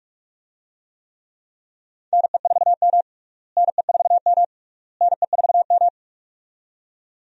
Morse code, heard as a keyed tone, about 42 wpm, 700 Hz. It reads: NE4M NE4M NE4M
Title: USA Call Signs - 4 Characters - Repeated 3x - 45wpm